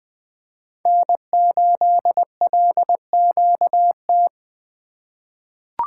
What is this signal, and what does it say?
Morse code, heard as a keyed tone, about 20 wpm, 700 Hz. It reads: N8LQT
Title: USA Call Signs - 5 Characters - 4x Character Spacing - 20wpm